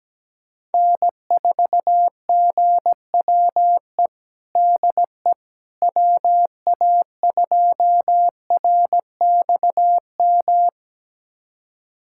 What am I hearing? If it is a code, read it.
Morse code, heard as a keyed tone, about 17 wpm, 700 Hz. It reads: N4GWE DE WA2RXM
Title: USA Callsign DE Callsign 17wpm